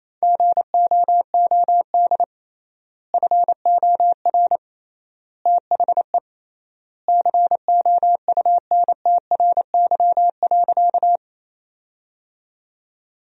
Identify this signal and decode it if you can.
Morse code, heard as a keyed tone, about 28 wpm, 700 Hz. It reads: GOOD FOR THE COUNTRY.
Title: 3x Word-Spacing - Sentences from Top 300 Words - 28wpm